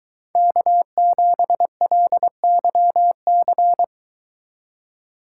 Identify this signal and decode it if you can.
Morse code, heard as a keyed tone, about 22 wpm, 700 Hz. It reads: K7LYC